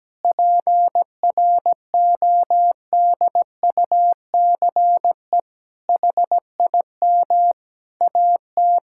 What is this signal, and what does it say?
Morse code, heard as a keyed tone, about 17 wpm, 700 Hz. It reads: PRODUCE HIM AT
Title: Sentences from Top 1000 Words - Review Part 5 - 17wpm